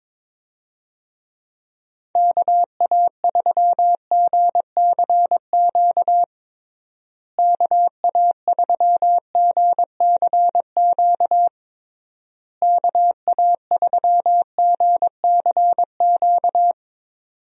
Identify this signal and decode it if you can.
Morse code, heard as a keyed tone, about 22 wpm, 700 Hz. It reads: KA3GCQ KA3GCQ KA3GCQ
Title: USA Call Signs - 6 Characters - Repeated 3x - 22wpm